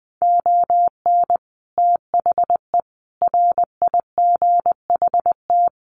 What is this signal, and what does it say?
Morse code, heard as a keyed tone, about 20 wpm, 700 Hz. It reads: ON THE RIGHT
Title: Sets of 5 Words Encore 20wpm